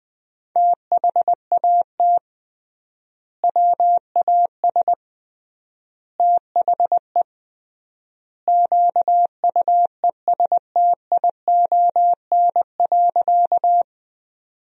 Morse code, heard as a keyed tone, about 20 wpm, 700 Hz. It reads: THAT WAS THE QUESTION.
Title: 3x Word-Spacing - Sentences from Top 400 Words - 20wpm